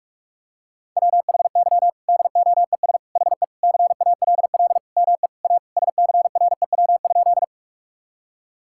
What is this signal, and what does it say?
Morse code, heard as a keyed tone, about 46 wpm, 700 Hz. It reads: WHY DOES HE CALL ME A SCREW?
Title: Sentences from Top 1600 Words 45wpm